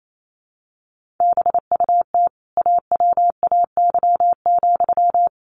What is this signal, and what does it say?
Morse code, heard as a keyed tone, about 28 wpm, 700 Hz. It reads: BUT AWAY,
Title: Sentences from Top 1400 Words 28wpm